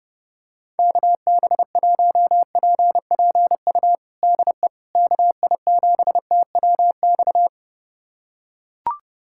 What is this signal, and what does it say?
Morse code, heard as a keyed tone, about 30 wpm, 700 Hz. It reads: KB1PPU DE KI7TWX
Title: USA Callsign DE Callsign - 6 Characters - 30wpm